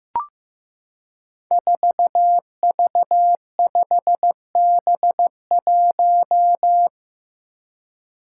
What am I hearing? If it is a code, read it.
Morse code, heard as a keyed tone, about 15 wpm, 700 Hz. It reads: 4V5B1
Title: Five Challenging Letter-Number Combinations 15wpm